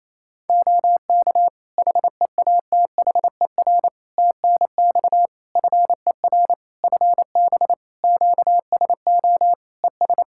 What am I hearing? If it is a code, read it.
Morse code, heard as a keyed tone, about 27 wpm, 700 Hz. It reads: OK HEATHER TNX FER FB QSO ES
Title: K1USN SST QSO Practice 28wpm